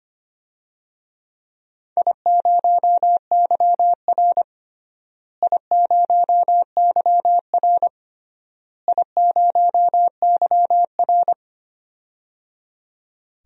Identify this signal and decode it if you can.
Morse code, heard as a keyed tone, about 25 wpm, 700 Hz. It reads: I0YR I0YR I0YR
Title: DXCC-100 Call Signs - 4 Characters - Repeated 3x - 25wpm